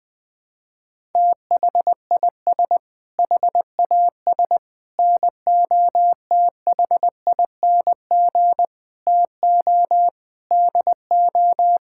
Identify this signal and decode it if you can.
Morse code, heard as a keyed tone, about 20 wpm, 700 Hz. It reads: THIS HAS NOTHING TO DO
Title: Sentences with 5-Word Phrases - 20wpm